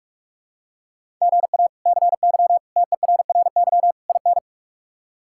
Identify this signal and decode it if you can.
Morse code, heard as a keyed tone, about 48 wpm, 700 Hz. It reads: GA CY TERRY IN
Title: K1USN SST QSO Practice - 45wpm